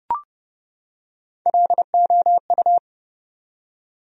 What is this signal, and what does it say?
Morse code, heard as a keyed tone, about 30 wpm, 700 Hz. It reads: LOU
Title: USA Common Names - Max 3 Letters - 30wpm